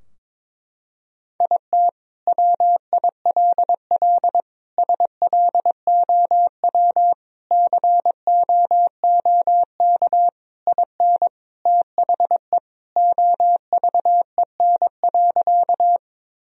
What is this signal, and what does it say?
Morse code, heard as a keyed tone, about 22 wpm, 700 Hz. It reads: IT WILL SLOW COOK IN THE OVEN.
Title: Sentences from Top 1000 Words - Review Part 4 - 22wpm